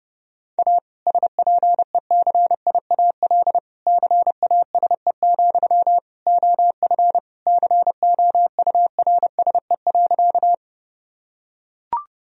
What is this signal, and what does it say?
Morse code, heard as a keyed tone, about 30 wpm, 700 Hz. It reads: A SPECIAL CASE, OF COURSE.
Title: Sentences from Top 1000 Words - Review Part 3 - 30wpm